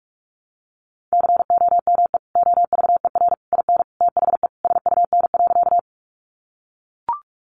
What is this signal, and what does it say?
Morse code, heard as a keyed tone, about 44 wpm, 700 Hz. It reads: COME OVER IN THE SUN.
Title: Sentences from Top 300 Words 45wpm